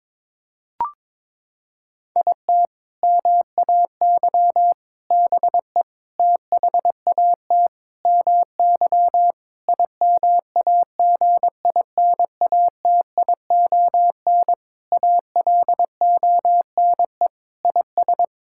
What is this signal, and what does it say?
Morse code, heard as a keyed tone, about 22 wpm, 700 Hz. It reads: IT MAY BE THAT MY IMAGINATION ALONE IS
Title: Sentences from Top 1600 Words 22wpm